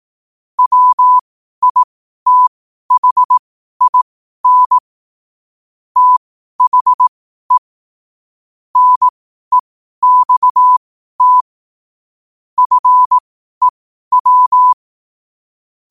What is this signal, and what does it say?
Morse code, heard as a keyed tone, about 18 wpm, spaced as for 12 wpm, 1000 Hz. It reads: WITHIN THE NEXT FEW
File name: qotdfiles / 12wpm